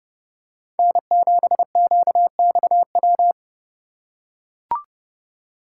Morse code, heard as a keyed tone, about 27 wpm, 700 Hz. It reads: N7QXW